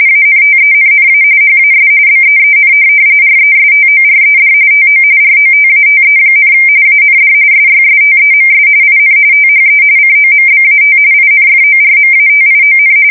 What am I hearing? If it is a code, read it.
RTTY, about 45.45 baud. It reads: ANTENNA HAS BEEN RESTORED TO ROTATE, AND SYSTEMS IN THE COMBAT INFORMATION
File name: fsk170 / USS-New-Jersey